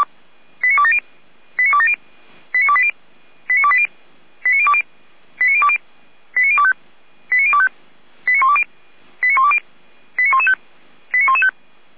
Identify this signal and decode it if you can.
Five-tone selective call (ZVEI), heard as quick runs of five tones, tones of about 70 ms. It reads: 89270 89270 89280 89280 89020 89020 89025 89025 89120 89120 89115 89115
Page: ZVEI